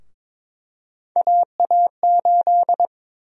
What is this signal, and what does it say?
Morse code, heard as a keyed tone, about 22 wpm, 700 Hz. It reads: AA8
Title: USA Call Sign Prefixes - Repeated 2x - 22wpm